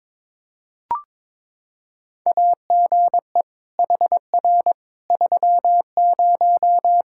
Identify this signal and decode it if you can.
Morse code, heard as a keyed tone, about 21 wpm, 700 Hz. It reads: AGE HR 30
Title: Practice QSO - Exchange 3 - More Info - 22wpm